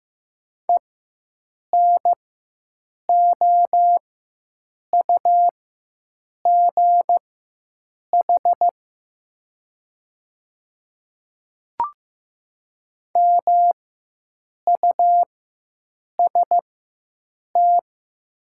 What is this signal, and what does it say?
Morse code, heard as a keyed tone, about 15 wpm, spaced as for 7 wpm, 700 Hz. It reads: ENOUGH MUST
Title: 4x Character Spacing - Top 500 Words - 15wpm